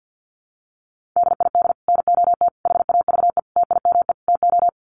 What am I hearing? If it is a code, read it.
Morse code, heard as a keyed tone, about 50 wpm, 700 Hz. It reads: DID NOT HAVE TIME TO
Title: Sets of 5 Words Encore 50wpm